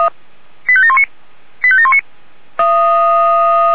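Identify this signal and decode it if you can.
Five-tone selective call (ZVEI), heard as quick runs of five tones, tones of about 70 ms. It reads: 85719 85719
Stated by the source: ZVEI five-tone selective call received on SDR.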